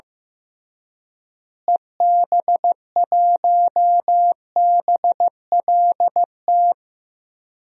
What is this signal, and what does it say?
Morse code, heard as a keyed tone, about 15 wpm, 700 Hz. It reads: EB1BLT
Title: DXCC-100 Call Signs - 6 Characters - Repeated 3x - 15wpm